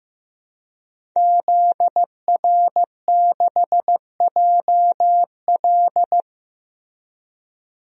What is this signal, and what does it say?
Morse code, heard as a keyed tone, about 15 wpm, 700 Hz. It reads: ZR6JL
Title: DXCC-100 Call Signs - 5 Characters - Repeated 2x - 15wpm